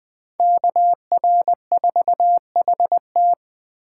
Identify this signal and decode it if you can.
Morse code, heard as a keyed tone, about 20 wpm, 700 Hz. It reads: KR4HT